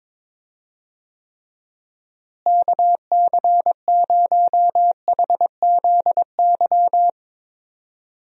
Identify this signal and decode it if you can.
Morse code, heard as a keyed tone, about 21 wpm, 700 Hz. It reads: KC0HZY
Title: USA Call Signs - 6 Characters - Repeated 2x - 22wpm